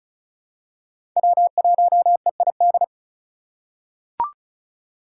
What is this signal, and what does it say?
Morse code, heard as a keyed tone, about 36 wpm, 700 Hz. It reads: W1EID
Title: USA Call Signs - 5 Characters - 4x Character Spacing - 35wpm